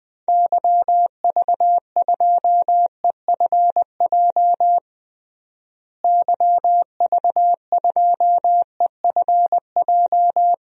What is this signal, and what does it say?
Morse code, heard as a keyed tone, about 20 wpm, 700 Hz. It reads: YV2EFJ YV2EFJ
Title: DXCC-100 Call Signs - 6 Characters - Repeated 2x - 20wpm